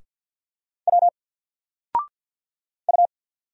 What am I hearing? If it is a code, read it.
Morse code, heard as a keyed tone, about 50 wpm, 700 Hz. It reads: W U
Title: Single Letter-Number 50wpm